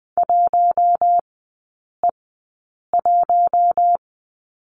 Morse code, heard as a keyed tone, about 20 wpm, spaced as for 8 wpm, 700 Hz. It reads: 1E1
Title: 041 Lesson 8e - Number 1 - Three Characters 20wpm